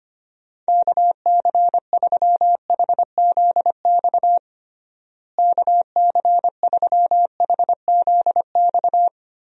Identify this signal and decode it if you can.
Morse code, heard as a keyed tone, about 25 wpm, 700 Hz. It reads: KC3HZX KC3HZX
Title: USA Call Signs - 6 Characters - Repeated 3x - 25wpm